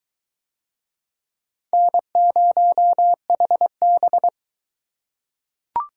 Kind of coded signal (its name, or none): Morse code